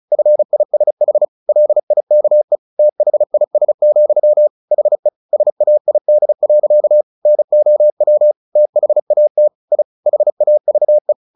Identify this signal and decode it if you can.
Morse code, heard as a keyed tone, about 35 wpm, 600 Hz. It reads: FISH LIKE THIS, HE SAID. NOW THAT I HAVE
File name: the old man and the sea 2 35wpm